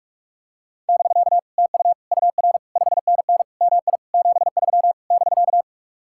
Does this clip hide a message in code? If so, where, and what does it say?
Yes: Morse code from 0.9 to 5.6 s, heard as a keyed tone, about 49 wpm, 700 Hz. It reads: <BK> TU UR 5NN MI 73 <BK>